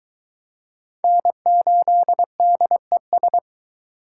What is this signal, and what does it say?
Morse code, heard as a keyed tone, about 22 wpm, 700 Hz. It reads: N8DES